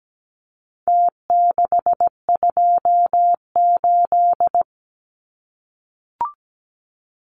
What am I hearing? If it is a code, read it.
Morse code, heard as a keyed tone, about 17 wpm, 700 Hz. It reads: T628
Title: Numbers - Contest Style - 4-Digit 17wpm